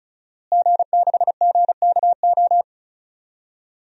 Morse code, heard as a keyed tone, about 36 wpm, 700 Hz. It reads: G6GKO
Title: DXCC-100 Call Signs - 5 Characters - Repeated 3x - 35wpm